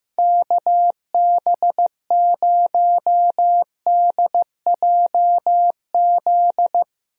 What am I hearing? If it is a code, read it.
Morse code, heard as a keyed tone, about 15 wpm, 700 Hz. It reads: KB0DJZ
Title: USA Call Signs - 6 Characters - 15wpm